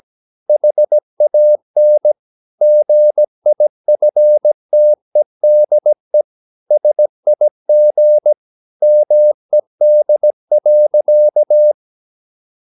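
Morse code, heard as a keyed tone, about 17 wpm, 600 Hz. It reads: HAN GIFTEDE SIG MED.
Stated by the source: Slow (QRS) Morse practice at 17 wpm, from CW med Gnister.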